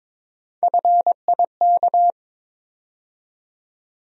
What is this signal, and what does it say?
Morse code, heard as a keyed tone, about 22 wpm, 700 Hz. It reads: FIK
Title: USA Call Sign Postfixes - 3 Letters 22wpm